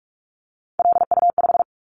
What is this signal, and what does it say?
Morse code, heard as a keyed tone, about 44 wpm, 700 Hz. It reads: LU5
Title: Callsign Prefixes for Top 100 DXCC - 45wpm